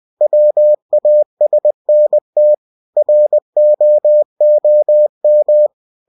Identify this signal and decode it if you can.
Morse code, heard as a keyed tone, about 20 wpm, 600 Hz. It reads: WASNT ROOM
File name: cw-books / winnie-the-pooh / winnie-20wpm-page-0093